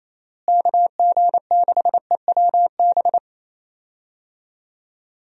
Morse code, heard as a keyed tone, about 28 wpm, 700 Hz. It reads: KG6EWB
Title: USA Call Signs - 6 Characters - Repeated 3x - 28wpm